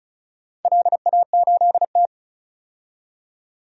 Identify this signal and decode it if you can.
Morse code, heard as a keyed tone, about 36 wpm, 700 Hz. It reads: LA8T
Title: DXCC-100 Call Signs - 4 Characters - 35wpm